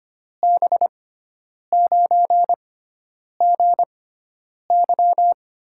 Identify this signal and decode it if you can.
Morse code, heard as a keyed tone, about 25 wpm, spaced as for 9 wpm, 700 Hz. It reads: B9GY
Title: USA Call Signs - 6 Characters - 6x Character Spacing - 25wpm